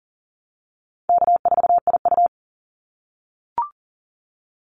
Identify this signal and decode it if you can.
Morse code, heard as a keyed tone, about 41 wpm, 700 Hz. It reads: K4IU
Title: Call Signs - 40wpm